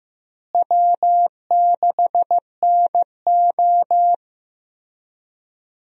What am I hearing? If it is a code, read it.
Morse code, heard as a keyed tone, about 16 wpm, 700 Hz. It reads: W6NO
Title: USA Call Signs - 4 Characters - Repeated 3x - 15wpm